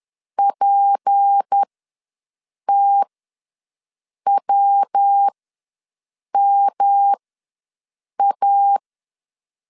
Morse code, heard as a keyed tone, about 10 wpm, spaced as for 6 wpm, 800 Hz. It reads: PTWMA
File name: LEZIONI / 7 GF9Z / 7 G ws 10wpm 3843